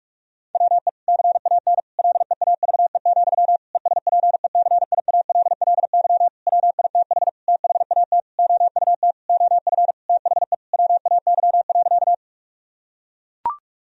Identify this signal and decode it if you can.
Morse code, heard as a keyed tone, about 41 wpm, 700 Hz. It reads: WE CAN LEAVE, ESPECIALLY WITH THAT OUT OF THE WAY.
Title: Sentences from Top 1000 Words - Review Part 4 - 45wpm